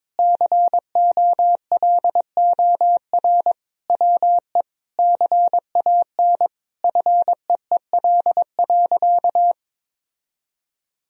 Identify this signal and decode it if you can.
Morse code, heard as a keyed tone, about 22 wpm, 700 Hz. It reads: COLOR WE CAN FEEL.
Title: Sentences from Top 1000 Words - Review Part 3 - 22wpm